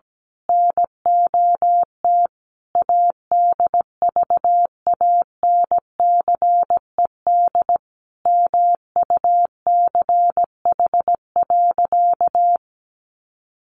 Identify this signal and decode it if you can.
Morse code, heard as a keyed tone, about 17 wpm, 700 Hz. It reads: NOT ADVANCED MUCH.
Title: Sentences from Top 1200 Words 17wpm